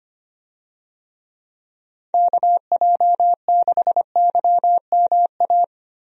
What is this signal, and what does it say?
Morse code, heard as a keyed tone, about 25 wpm, 700 Hz. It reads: KJ6YMA
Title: USA Call Signs - 6 Characters - Repeated 3x - 25wpm